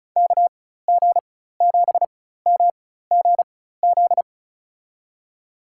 Morse code, heard as a keyed tone, about 36 wpm, spaced as for 16 wpm, 700 Hz. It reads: KG7MGZ
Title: USA Call Signs - 6 Characters - 4x Character Spacing - 35wpm